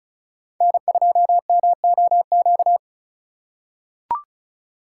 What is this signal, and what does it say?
Morse code, heard as a keyed tone, about 36 wpm, 700 Hz. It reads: N2MOQ
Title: USA Call Signs - 5 Characters - 4x Character Spacing - 35wpm